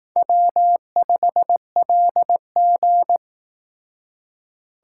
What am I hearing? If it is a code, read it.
Morse code, heard as a keyed tone, about 18 wpm, 700 Hz. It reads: W5LG
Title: Speed Racing - USA Call Signs 15wpm